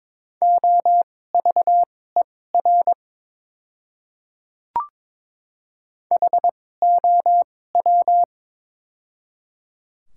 Morse code, heard as a keyed tone, about 22 wpm, spaced as for 16 wpm, 690 Hz. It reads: OVER HOW
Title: 2x Character Spacing - Top 100 Words - 22wpm